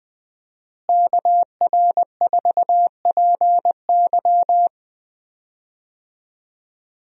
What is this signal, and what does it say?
Morse code, heard as a keyed tone, about 20 wpm, 700 Hz. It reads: KR4PY
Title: USA Call Signs - 5 Characters - Repeated 2x - 20wpm